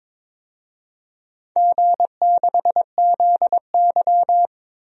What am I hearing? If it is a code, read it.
Morse code, heard as a keyed tone, about 22 wpm, 700 Hz. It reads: G6ZY
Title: DXCC-100 Call Signs - 4 Characters - Repeated 2x - 22wpm